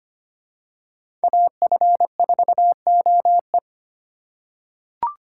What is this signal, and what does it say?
Morse code, heard as a keyed tone, about 25 wpm, 700 Hz. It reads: AF4OE
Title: USA Call Signs - 5 Characters - 3x Character Spacing - 25wpm